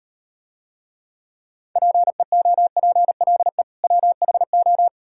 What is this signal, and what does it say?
Morse code, heard as a keyed tone, about 36 wpm, 700 Hz. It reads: PEOPLE WHO